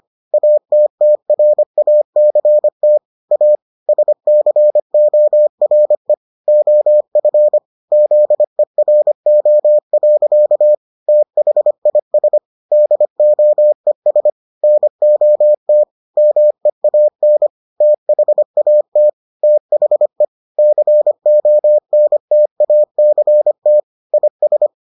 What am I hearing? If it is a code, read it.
Morse code, heard as a keyed tone, about 25 wpm, 600 Hz. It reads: ATTRACT A SCORE OF ZERO. THIS DOES NOT MEAN THAT THE CONTACT IS